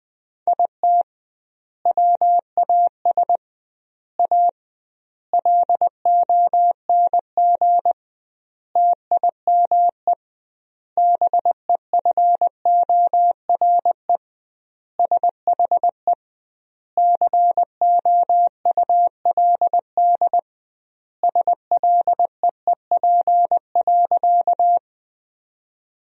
Morse code, heard as a keyed tone, about 21 wpm, 700 Hz. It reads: IT WAS A LONG TIME BEFORE SHE COULD SLEEP.
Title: 2x Word-Spacing - Sentences from Top 500 Words - 20wpm